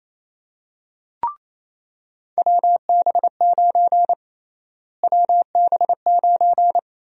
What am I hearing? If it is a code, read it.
Morse code, heard as a keyed tone, about 28 wpm, 700 Hz. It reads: WB9 WB9
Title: USA Call Sign Prefixes - Repeated 2x - 28wpm